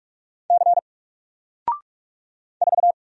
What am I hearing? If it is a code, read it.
Morse code, heard as a keyed tone, about 44 wpm, 700 Hz. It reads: C 4
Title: Single Letter-Number Alt 45wpm